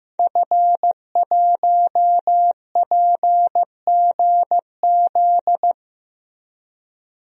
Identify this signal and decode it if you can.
Morse code, heard as a keyed tone, about 15 wpm, 700 Hz. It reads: F1PGZ
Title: DXCC-100 Call Signs - 5 Characters - Repeated 2x - 15wpm